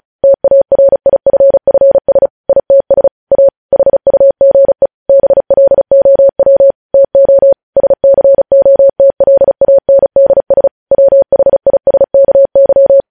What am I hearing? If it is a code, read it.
Morse code, heard as a keyed tone, about 36 wpm, 550 Hz. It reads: TARIFFS ITS A HUGE BLOW TO SCOTLANDS WHISKY
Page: news headlines in morse code at 35 wpm